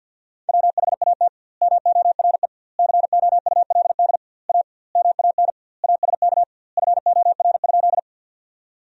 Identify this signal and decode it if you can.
Morse code, heard as a keyed tone, about 50 wpm, 700 Hz. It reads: WHAT MORE COULD A MAN ASK FOR?
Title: 2x Word-Spacing - Sentences from Top 200 Words - 50wpm